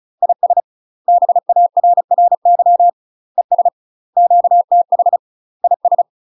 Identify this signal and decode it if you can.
Morse code, heard as a keyed tone, about 34 wpm, 700 Hz. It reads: IS BARRY ES QTH IS
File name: UK MASTER QSOs TRAINING / 35WPM / UK MASTER QSOs 0005